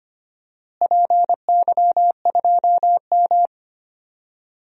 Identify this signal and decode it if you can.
Morse code, heard as a keyed tone, about 25 wpm, 700 Hz. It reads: PY2M
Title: DXCC-100 Call Signs - 4 Characters - Repeated 2x - 25wpm